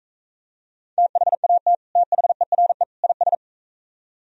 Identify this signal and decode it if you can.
Morse code, heard as a keyed tone, about 43 wpm, 700 Hz. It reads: THAT THERE IS